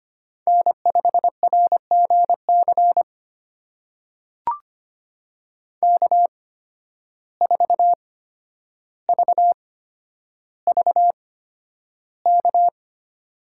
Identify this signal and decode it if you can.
Morse code, heard as a keyed tone, about 25 wpm, spaced as for 11 wpm, 700 Hz. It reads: N5RGC K4VVK
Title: USA Call Signs - 5 Characters - 8x Character Spacing - 25wpm